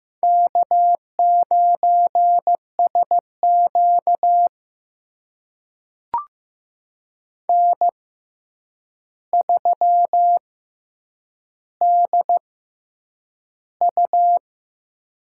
Morse code, heard as a keyed tone, about 15 wpm, spaced as for 8 wpm, 700 Hz. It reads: K9SQ N3DU
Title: USA Call Signs - 4 Characters - 6x Character Spacing - 15wpm